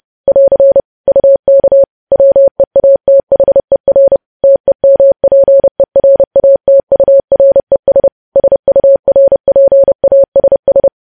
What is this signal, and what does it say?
Morse code, heard as a keyed tone, about 30 wpm, 550 Hz. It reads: <AR> UK WEATHER TEMPERATURES SURPASS
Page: news headlines in morse code at 30 wpm